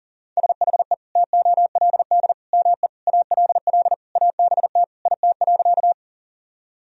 Morse code, heard as a keyed tone, about 40 wpm, 700 Hz. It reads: SHE TOLD ME ALL ABT IT.
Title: Sentences from Top 400 Words and Abbreviations 40wpm